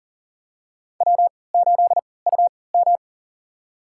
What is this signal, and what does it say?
Morse code, heard as a keyed tone, about 38 wpm, spaced as for 22 wpm, 700 Hz. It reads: W8UM
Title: USA Call Signs - 4 Characters - 3x Character Spacing - 40wpm